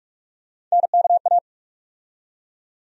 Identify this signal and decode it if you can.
Morse code, heard as a keyed tone, about 47 wpm, 700 Hz. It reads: NKA
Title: USA Call Sign Postfixes - 3 Letters 45wpm